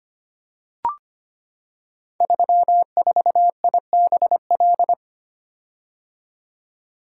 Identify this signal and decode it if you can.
Morse code, heard as a keyed tone, about 25 wpm, 690 Hz. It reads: 34IBL
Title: Code Groups - 5 Letter:Numbers - 25wpm